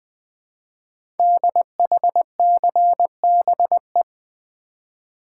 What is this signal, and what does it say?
Morse code, heard as a keyed tone, about 20 wpm, 700 Hz. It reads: DHCBE
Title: Code Groups - 5 Letters - 20wpm